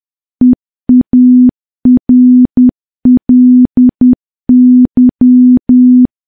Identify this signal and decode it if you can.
Morse code, heard as a keyed tone, about 10 wpm, 250 Hz. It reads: EARLY
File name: morse / morse-training-10WPM